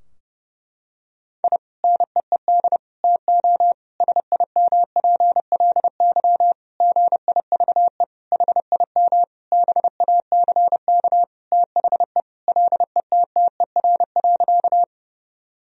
Morse code, heard as a keyed tone, about 30 wpm, 700 Hz. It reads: I NEED TO SIMPLY GIVE HIM BACK THE LETTER.
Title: Sentences from Top 1000 Words - Review Part 4 - 30wpm